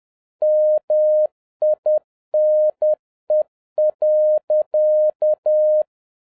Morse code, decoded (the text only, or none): MINE.